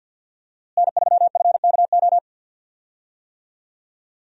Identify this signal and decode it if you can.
Morse code, heard as a keyed tone, about 47 wpm, 700 Hz. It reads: N3FKO
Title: USA Call Signs - 5 Characters - Repeated 2x - 50wpm